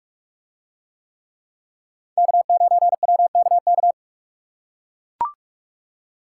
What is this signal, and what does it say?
Morse code, heard as a keyed tone, about 44 wpm, 700 Hz. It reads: K9WKK